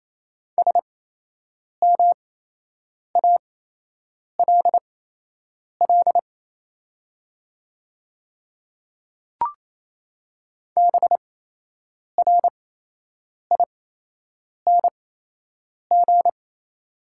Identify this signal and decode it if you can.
Morse code, heard as a keyed tone, about 28 wpm, spaced as for 8 wpm, 690 Hz. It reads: SMALL BRING
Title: 8x Character Spacing - Top 500 Words - 28wpm